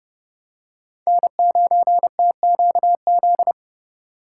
Morse code, heard as a keyed tone, about 31 wpm, 700 Hz. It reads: N9TQZ
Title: USA Call Signs - 5 Characters - 4x Character Spacing - 30wpm